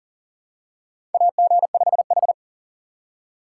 Morse code, heard as a keyed tone, about 38 wpm, 700 Hz. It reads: AG5H
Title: USA Call Signs - 4 Characters - 2x Character Spacing - 40wpm